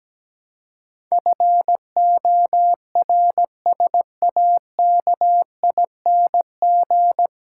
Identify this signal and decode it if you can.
Morse code, heard as a keyed tone, about 17 wpm, 700 Hz. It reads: FORSAKING
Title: Pangram Challenge v2 - 17wpm